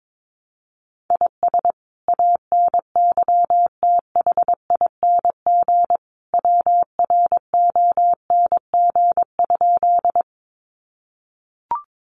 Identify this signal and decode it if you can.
Morse code, heard as a keyed tone, about 22 wpm, 700 Hz. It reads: IS ANYTHING WRONG?
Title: Sentences from Top 1000 Words 22wpm